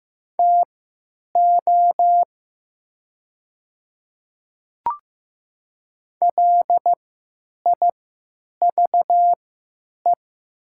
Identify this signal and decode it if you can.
Morse code, heard as a keyed tone, about 15 wpm, spaced as for 9 wpm, 700 Hz. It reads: TO LIVE
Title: 3x Character Spacing - Top 200 Words - 15wpm